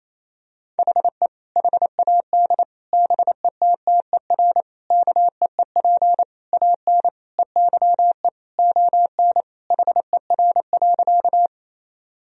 Morse code, heard as a keyed tone, about 28 wpm, 700 Hz. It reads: HE HAD BETTER KEEP AN EYE ON HER.
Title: Sentences from Top 1000 Words - Review Part 1 - 28wpm